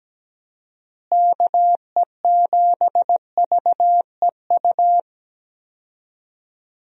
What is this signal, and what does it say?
Morse code, heard as a keyed tone, about 17 wpm, 700 Hz. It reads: KE7VEU
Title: USA Call Signs - 6 Characters - Repeated 2x - 17wpm